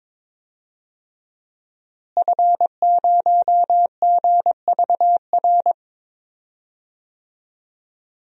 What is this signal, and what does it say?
Morse code, heard as a keyed tone, about 22 wpm, 700 Hz. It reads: F0GVR
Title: DXCC-100 Call Signs - 5 Characters - Repeated 3x - 22wpm